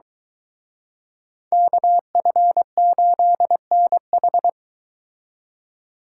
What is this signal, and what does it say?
Morse code, heard as a keyed tone, about 22 wpm, 700 Hz. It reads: KF8NH